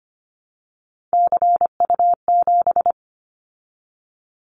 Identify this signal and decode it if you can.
Morse code, heard as a keyed tone, about 25 wpm, 700 Hz. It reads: CU7
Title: Callsign Prefixes for Top 50 DXCC - 25wpm